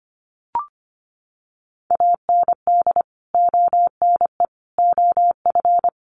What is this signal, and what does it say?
Morse code, heard as a keyed tone, about 25 wpm, 700 Hz. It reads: AND ONE OF
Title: Sets of 5 Words Encore 25wpm